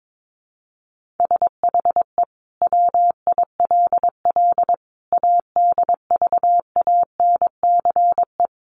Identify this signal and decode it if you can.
Morse code, heard as a keyed tone, about 22 wpm, 700 Hz. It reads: SHE WILL ADVANCE
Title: Sentences from Top 1400 Words 22wpm